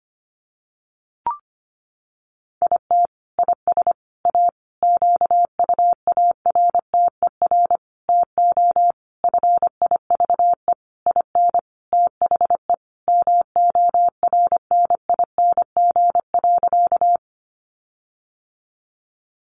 Morse code, heard as a keyed tone, about 25 wpm, 700 Hz. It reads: IT IS A QUARTER TO FIVE IN THE MORNING.
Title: Sentences from Top 1300 Words 25wpm